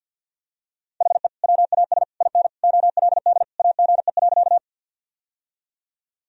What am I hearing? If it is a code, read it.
Morse code, heard as a keyed tone, about 50 wpm, 700 Hz. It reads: HE WAS IN OLD AGE.
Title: Sentences from Top 1000 Words - Review Part 3 - 50wpm